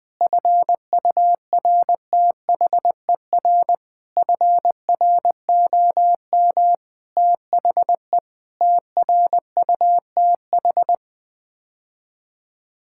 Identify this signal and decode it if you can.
Morse code, heard as a keyed tone, about 20 wpm, 700 Hz. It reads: FURTHER FROM THE TRUTH
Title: Sets of 8 Words - 20wpm